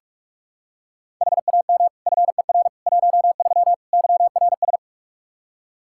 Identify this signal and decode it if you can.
Morse code, heard as a keyed tone, about 47 wpm, 700 Hz. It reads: HAM FER 13 YRS
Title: Practice QSO - Exchange 3 - More Info - 45wpm